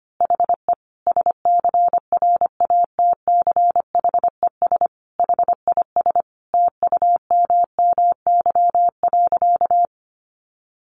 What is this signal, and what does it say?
Morse code, heard as a keyed tone, about 25 wpm, 700 Hz. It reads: HE SCRATCHES HIS TUMMY.
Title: Easy Phrases 25wpm